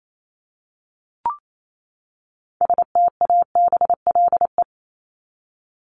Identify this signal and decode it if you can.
Morse code, heard as a keyed tone, about 28 wpm, 700 Hz. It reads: STABLE
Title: Top 1251-1500 Words 28wpm